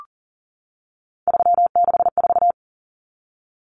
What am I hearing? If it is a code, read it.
Morse code, heard as a keyed tone, about 40 wpm, 700 Hz. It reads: 364
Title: Numbers - 3-Digit 40wpm